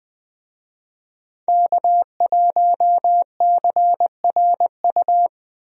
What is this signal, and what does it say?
Morse code, heard as a keyed tone, about 20 wpm, 700 Hz. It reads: K1CRU
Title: USA Call Signs - 5 Characters - Repeated 3x - 20wpm